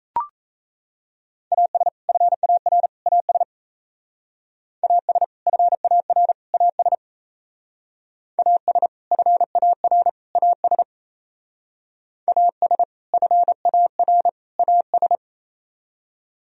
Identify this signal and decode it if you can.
Morse code, heard as a keyed tone, about 34 wpm, 700 Hz. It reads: AS FAR AS AS FAR AS AS FAR AS AS FAR AS
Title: Speed-Racing - Sets of 3 Words 28wpm